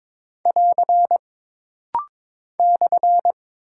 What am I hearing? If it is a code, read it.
Morse code, heard as a keyed tone, about 22 wpm, 700 Hz. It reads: <AR> /
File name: practice / 22wpm / Single Punctuation-Prosign - Rapid-Fire - 22wpm